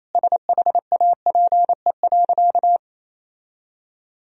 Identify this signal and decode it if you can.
Morse code, heard as a keyed tone, about 28 wpm, 700 Hz. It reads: SHAPE.
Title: Sentences from Top 1000 Words - Review Part 3 - 28wpm